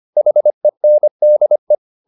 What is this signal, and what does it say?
Morse code, heard as a keyed tone, about 25 wpm, 600 Hz. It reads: HENDE
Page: Ferie 25 wpm